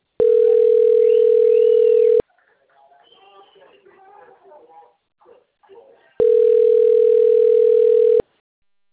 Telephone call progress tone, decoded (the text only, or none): ringback tone